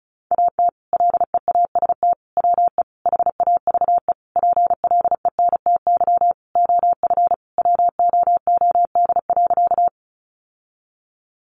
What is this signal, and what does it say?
Morse code, heard as a keyed tone, about 34 wpm, 700 Hz. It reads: AT LEAST WE HAVE PLENTY OF WOOD.
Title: Sentences from Top 1300 Words 35wpm